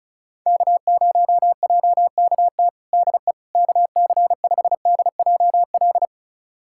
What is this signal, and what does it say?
Morse code, heard as a keyed tone, about 36 wpm, 700 Hz. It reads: K0JKT DE KC5DJL
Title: USA Callsign DE Callsign 35wpm